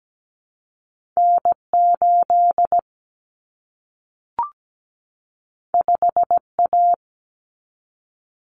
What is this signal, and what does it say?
Morse code, heard as a keyed tone, about 17 wpm, 700 Hz. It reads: N 8 5 A
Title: Common Cut Numbers - 2-Digit 17wpm